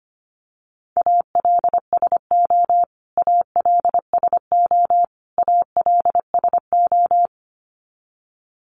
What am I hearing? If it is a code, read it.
Morse code, heard as a keyed tone, about 25 wpm, 700 Hz. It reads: ALSO ALSO ALSO
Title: Top 500 Words Repeated 25wpm